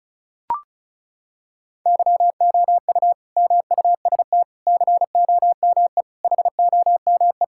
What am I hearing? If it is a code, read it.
Morse code, heard as a keyed tone, about 36 wpm, 700 Hz. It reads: YOU MUST COME HOME
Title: Sentences from Top 1000 Words - Review Part 5 - 35wpm